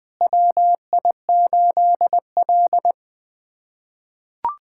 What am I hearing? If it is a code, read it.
Morse code, heard as a keyed tone, about 20 wpm, 700 Hz. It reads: WI8L
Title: USA Callsign DE Callsign - 4 Characters - 20wpm